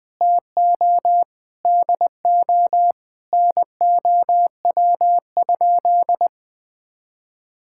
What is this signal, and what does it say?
Morse code, heard as a keyed tone, about 20 wpm, 700 Hz. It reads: TO DO NOW?
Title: Sentences with 3-Word Phrases - 20wpm